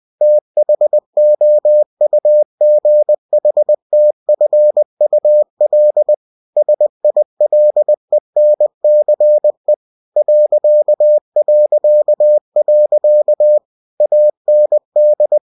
Morse code, heard as a keyed tone, about 20 wpm, 600 Hz. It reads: THOUGHTFUL SILENCE ... AND
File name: cw-books / winnie-the-pooh / winnie-20wpm-page-0060